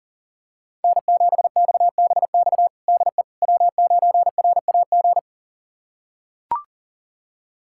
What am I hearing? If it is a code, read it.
Morse code, heard as a keyed tone, about 40 wpm, 700 Hz. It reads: N7XBX DE W9RAG
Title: USA Callsign DE Callsign - 5 Characters - 40wpm